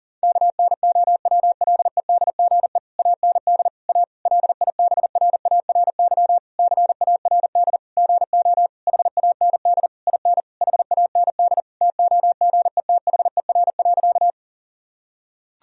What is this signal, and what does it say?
Morse code, heard as a keyed tone, about 37 wpm, 700 Hz. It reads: KNOWLEDGE AND A LIBRARY CARD GO HAND IN HAND TOGETHER.
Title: Sentences from Top 1000 Words - Review Part 4 - 40wpm